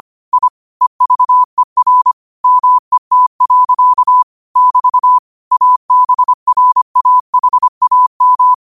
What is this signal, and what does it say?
Morse code, heard as a keyed tone, about 25 wpm, 1000 Hz. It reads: I EVER MET. <BT> ABRAHAM
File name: qotdfiles / 25wpm